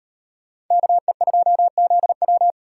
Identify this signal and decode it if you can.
Morse code, heard as a keyed tone, about 36 wpm, 700 Hz. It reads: KE2ZW